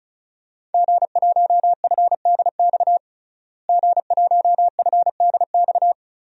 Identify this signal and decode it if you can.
Morse code, heard as a keyed tone, about 36 wpm, 700 Hz. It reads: G1FDX G1FDX
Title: DXCC-100 Call Signs - 5 Characters - Repeated 3x - 35wpm